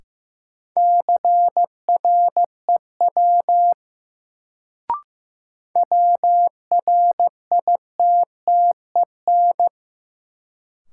Morse code, heard as a keyed tone, about 15 wpm, 700 Hz. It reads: CREW WRITTEN
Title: Sentences from Top 1500 Words - New Word Drill - 15wpm